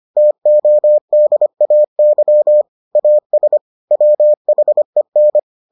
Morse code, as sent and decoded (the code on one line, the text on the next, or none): - --- -.. .- -.-- / .- ... / .-- .... -.
TODAY AS WHEN